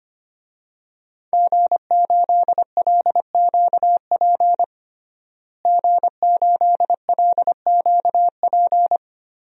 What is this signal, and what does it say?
Morse code, heard as a keyed tone, about 25 wpm, 700 Hz. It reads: G8LQP G8LQP
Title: DXCC-100 Call Signs - 5 Characters - Repeated 2x - 25wpm